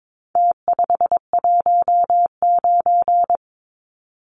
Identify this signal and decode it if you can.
Morse code, heard as a keyed tone, about 22 wpm, 700 Hz. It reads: T519
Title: Numbers - Contest Style - 4-Digit 22wpm